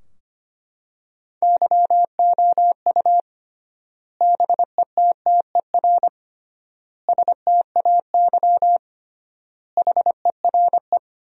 Morse code, heard as a keyed tone, about 25 wpm, 700 Hz. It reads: YOU BETTER STAY HERE
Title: 3x Word-Spacing - Sentences from Top 500 Words - 25wpm